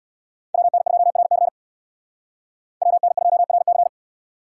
Morse code, heard as a keyed tone, about 77 wpm, 700 Hz. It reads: WN2RC WN2RC